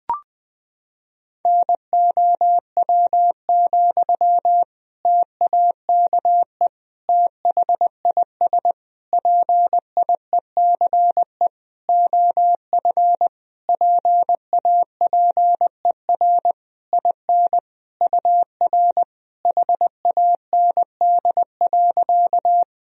Morse code, heard as a keyed tone, about 20 wpm, 700 Hz. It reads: NOW, TAKE THIS PIECE OF PAPER IN UR HAND.
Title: Sentences from Top 400 Words and Abbreviations 20wpm